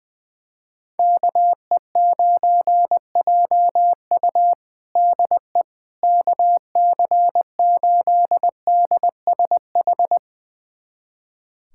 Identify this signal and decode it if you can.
Morse code, heard as a keyed tone, about 20 wpm, 700 Hz. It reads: KE9JU DE KC8DSH
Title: USA Callsign DE Callsign 20wpm